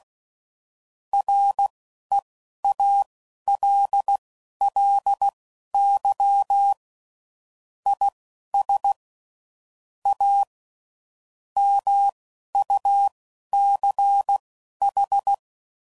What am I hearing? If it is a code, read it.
Morse code, heard as a keyed tone, about 16 wpm, spaced as for 11 wpm, 800 Hz. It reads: REALLY IS A MUCH
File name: CW Code Groups @10WPM / Technical Subjects / Gates Intro